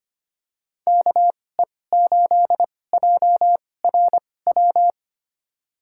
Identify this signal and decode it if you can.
Morse code, heard as a keyed tone, about 25 wpm, spaced as for 18 wpm, 700 Hz. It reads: KE8JRW
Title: USA Call Signs - 6 Characters - 2x Character Spacing - 25wpm